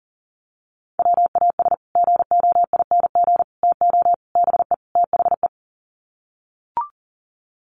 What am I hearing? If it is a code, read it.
Morse code, heard as a keyed tone, about 40 wpm, 700 Hz. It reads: WAS GOING TO BE THE
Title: Sets of 5 Words Encore2 40wpm